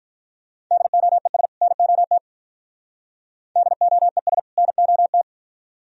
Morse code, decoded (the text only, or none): DOES NOT DOES NOT